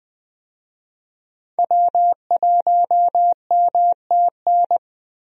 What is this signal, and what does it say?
Morse code, heard as a keyed tone, about 21 wpm, 700 Hz. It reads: W1MTN
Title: USA Call Signs - 5 Characters - Repeated 3x - 20wpm